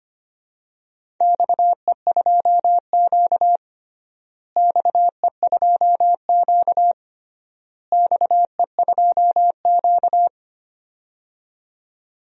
Morse code, heard as a keyed tone, about 25 wpm, 700 Hz. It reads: XE2Q XE2Q XE2Q
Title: DXCC-100 Call Signs - 4 Characters - Repeated 3x - 25wpm